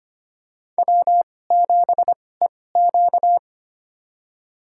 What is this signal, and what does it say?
Morse code, heard as a keyed tone, about 25 wpm, spaced as for 18 wpm, 700 Hz. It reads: W7EQ
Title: USA Call Signs - 2x Character Spacing - 25wpm